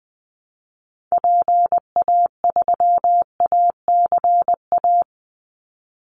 Morse code, heard as a keyed tone, about 20 wpm, 700 Hz. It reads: PA3ACA
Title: Call Signs - 20wpm